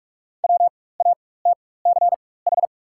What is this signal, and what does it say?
Morse code, heard as a keyed tone, about 43 wpm, spaced as for 21 wpm, 690 Hz. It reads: WATCH
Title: 4x Character Spacing - Top 500 Words - 45wpm